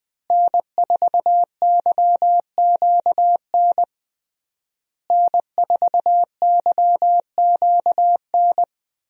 Morse code, heard as a keyed tone, about 21 wpm, 700 Hz. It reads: N4YQN N4YQN
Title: USA Call Signs - 5 Characters - Repeated 3x - 20wpm